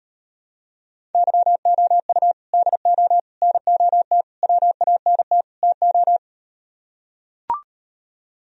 Morse code, heard as a keyed tone, about 38 wpm, 700 Hz. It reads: YOU DO NOT WANT TO